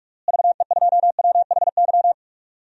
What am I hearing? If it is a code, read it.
Morse code, heard as a keyed tone, about 41 wpm, 700 Hz. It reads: VE2WHY
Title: DXCC-100 Call Signs - 6 Characters - Repeated 3x - 45wpm